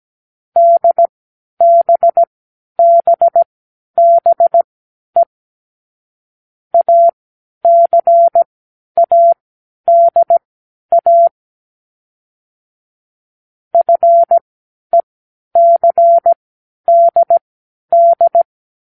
Morse code, heard as a keyed tone, about 17 wpm, spaced as for 10 wpm, 680 Hz. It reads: DBBBE ACADA FECDD
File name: k7qo / Lesson9(A-F)